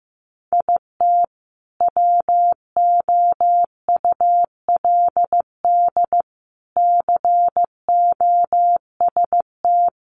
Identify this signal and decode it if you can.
Morse code, heard as a keyed tone, about 15 wpm, 700 Hz. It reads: IT WOULD COST
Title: Sentences from Top 800 Words 15wpm